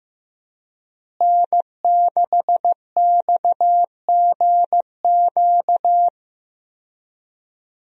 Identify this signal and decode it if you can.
Morse code, heard as a keyed tone, about 15 wpm, 700 Hz. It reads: N6XGQ
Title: USA Call Signs - 5 Characters - Repeated 3x - 15wpm